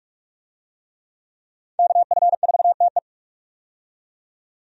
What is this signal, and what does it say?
Morse code, heard as a keyed tone, about 47 wpm, 700 Hz. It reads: KF4TE
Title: USA Call Signs - 5 Characters - 45wpm